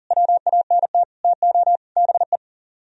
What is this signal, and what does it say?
Morse code, heard as a keyed tone, about 55 wpm, spaced as for 47 wpm, 700 Hz. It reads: WANT TO BE
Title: Speed-Racing - Sets of 4 Words 40wpm